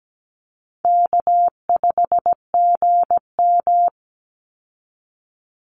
Morse code, heard as a keyed tone, about 17 wpm, 700 Hz. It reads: K5GM
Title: Easy Call Signs Repeated 3x - 17wpm